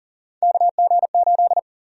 Morse code, heard as a keyed tone, about 40 wpm, 700 Hz. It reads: KG8
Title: USA Call Sign Prefixes - 40wpm